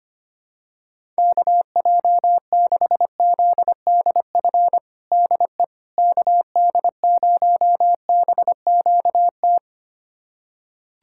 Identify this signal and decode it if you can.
Morse code, heard as a keyed tone, about 25 wpm, 700 Hz. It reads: KJ6ZDF DE KD0BQT
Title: USA Callsign DE Callsign - 6 Characters - 25wpm